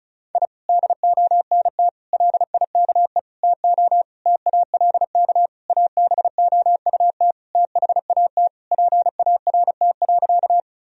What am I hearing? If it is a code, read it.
Morse code, heard as a keyed tone, about 36 wpm, 700 Hz. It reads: I DONT LIKE TO TALK ABOUT THAT PART.
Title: Sentences from Top 1000 Words - Review Part 1 - 35wpm